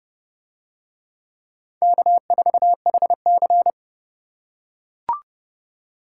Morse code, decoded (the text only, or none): K4HC